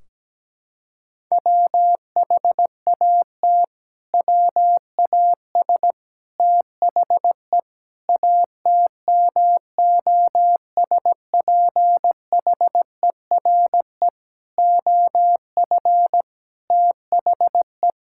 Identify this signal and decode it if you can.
Morse code, heard as a keyed tone, about 17 wpm, 700 Hz. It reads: WHAT WAS THE ATMOSPHERE OF THE
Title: Sentences from Top 1600 Words 17wpm